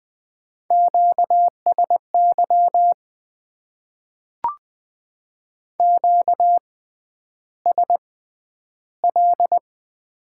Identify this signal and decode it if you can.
Morse code, heard as a keyed tone, about 20 wpm, spaced as for 10 wpm, 700 Hz. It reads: QSY QSL
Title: Top 90 QSO Elements - 6x Character Spacing - 20wpm